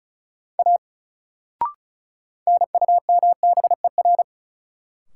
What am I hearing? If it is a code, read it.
Morse code, heard as a keyed tone, about 36 wpm, 700 Hz. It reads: A NUMBER
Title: Sentences from Top 100 Words - New Word Drill - 35wpm